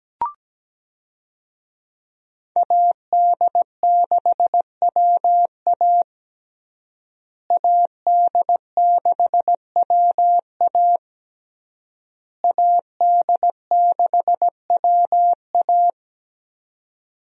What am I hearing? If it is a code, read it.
Morse code, heard as a keyed tone, about 17 wpm, 700 Hz. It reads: AD6WA AD6WA AD6WA
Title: USA Call Signs - 5 Characters - Repeated 3x - 17wpm